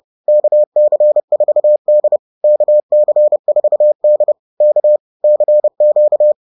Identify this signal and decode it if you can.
Morse code, heard as a keyed tone, about 30 wpm, 600 Hz. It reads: KC4D KC4D K CQ
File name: 30wpm